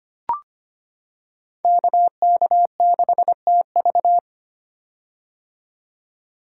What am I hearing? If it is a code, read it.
Morse code, heard as a keyed tone, about 25 wpm, 700 Hz. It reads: KK6TV
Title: K1USN SST QSO Practice - 25wpm